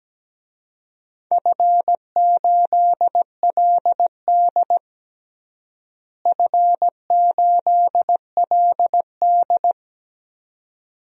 Morse code, heard as a keyed tone, about 17 wpm, 700 Hz. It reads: F8LD F8LD
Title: DXCC-100 Call Signs - 4 Characters - Repeated 2x - 17wpm